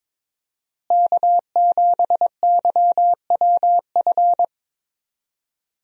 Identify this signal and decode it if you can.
Morse code, heard as a keyed tone, about 22 wpm, 700 Hz. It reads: K7YWF
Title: USA Call Signs - 5 Characters - Repeated 2x - 22wpm